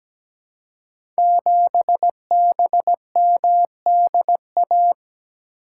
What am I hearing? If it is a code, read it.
Morse code, heard as a keyed tone, about 17 wpm, 700 Hz. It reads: 7BMDA
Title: Code Groups - 5 Letter:Numbers - 17wpm